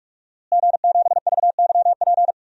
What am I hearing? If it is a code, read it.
Morse code, heard as a keyed tone, about 42 wpm, 700 Hz. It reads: G7VYP
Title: DXCC-100 Call Signs - 5 Characters - Repeated 2x - 45wpm